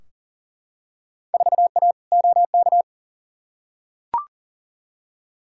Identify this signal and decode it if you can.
Morse code, heard as a keyed tone, about 40 wpm, 700 Hz. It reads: 4A OK
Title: Field Day Exchanges - 40wpm